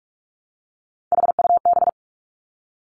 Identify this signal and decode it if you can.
Morse code, heard as a keyed tone, about 46 wpm, 700 Hz. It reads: HUB
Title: Three Challenging Letter-Number Combinations 45wpm